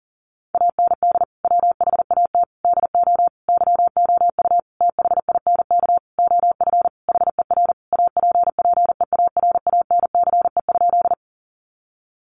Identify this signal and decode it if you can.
Morse code, heard as a keyed tone, about 40 wpm, 700 Hz. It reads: AND WHAT DO YOU THINK OF HER APPEARANCE?
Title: Sentences from Top 1100 Words 40wpm